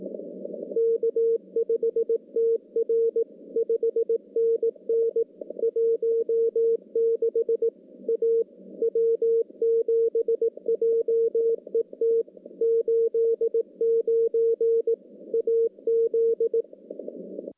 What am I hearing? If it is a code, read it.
Morse code, heard as a keyed tone, about 18 wpm, 460 Hz. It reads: K5TR 5NN 16 A W7JET 89 AZ
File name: contests / CW-SS-k5tr- -1300